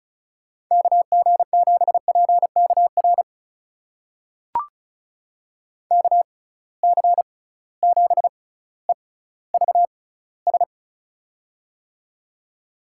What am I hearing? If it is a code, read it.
Morse code, heard as a keyed tone, about 34 wpm, spaced as for 18 wpm, 700 Hz. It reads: KG7PKR KC7EVS
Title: USA Call Signs - 6 Characters - 6x Character Spacing - 35wpm